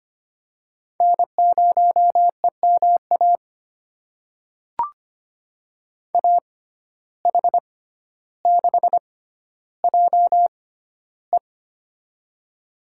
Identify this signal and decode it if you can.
Morse code, heard as a keyed tone, about 25 wpm, spaced as for 13 wpm, 690 Hz. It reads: N0EMA AH6JE
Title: USA Call Signs - 5 Characters - 6x Character Spacing - 25wpm